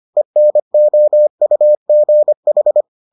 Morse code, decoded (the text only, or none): ENOUGH